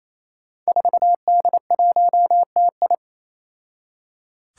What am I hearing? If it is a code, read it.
Morse code, heard as a keyed tone, about 28 wpm, 700 Hz. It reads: 4D1TI
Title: Five Challenging Letter-Number Combinations 28wpm